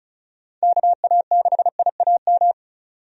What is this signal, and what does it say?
Morse code, heard as a keyed tone, about 36 wpm, 700 Hz. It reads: KA6IAM
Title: USA Call Signs - 6 Characters - 35wpm